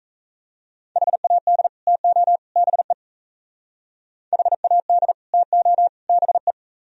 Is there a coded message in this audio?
Yes: Morse code from 1.0 to 6.5 s, heard as a keyed tone, about 36 wpm, 700 Hz. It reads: HAD TO BE HAD TO BE